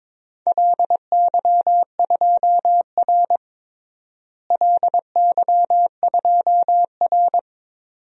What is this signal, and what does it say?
Morse code, heard as a keyed tone, about 22 wpm, 700 Hz. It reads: LY2R LY2R
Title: DXCC-100 Call Signs - 4 Characters - Repeated 2x - 22wpm